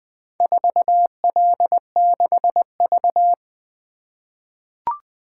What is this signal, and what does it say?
Morse code, heard as a keyed tone, about 20 wpm, 700 Hz. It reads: 4L6V
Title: Four Letter-Number Combinations - 20wpm